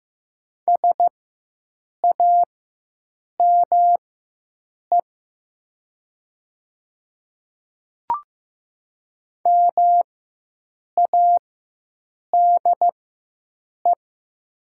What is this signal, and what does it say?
Morse code, heard as a keyed tone, about 15 wpm, spaced as for 7 wpm, 700 Hz. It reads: SAME MADE
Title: 4x Character Spacing - Top 200 Words - 15wpm